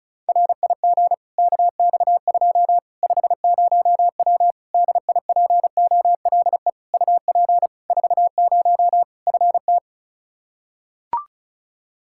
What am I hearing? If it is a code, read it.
Morse code, heard as a keyed tone, about 36 wpm, 700 Hz. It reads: RIG KX2 50W DIPOLE UP 40 FT
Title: Practice QSO - Exchange 2 - Info - 35wpm